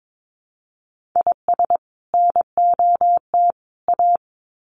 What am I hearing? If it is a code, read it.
Morse code, heard as a keyed tone, about 22 wpm, 700 Hz. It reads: IS NOT A
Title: Sets of 3 Words 22wpm